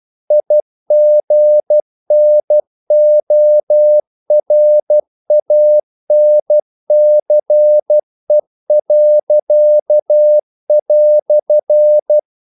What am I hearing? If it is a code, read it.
Morse code, heard as a keyed tone, about 12 wpm, 600 Hz. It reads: IGNORANCE."
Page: English quotations in CW (telegraphy - morse)